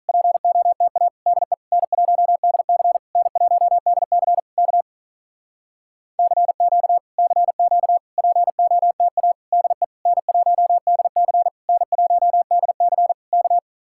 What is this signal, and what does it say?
Morse code, heard as a keyed tone, about 46 wpm, 700 Hz. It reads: POTA DE N1DC N1DC K CQ CQ POTA DE N1DC N1DC K